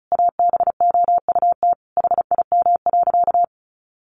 Morse code, heard as a keyed tone, about 34 wpm, 700 Hz. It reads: ABOUT HIM.
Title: Sentences from Top 200 Words 35wpm